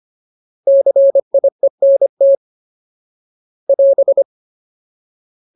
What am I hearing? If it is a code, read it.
Morse code, heard as a keyed tone, about 25 wpm, 550 Hz. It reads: CIENT <AS>
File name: Ultimate Word List / 25WPM Ultimate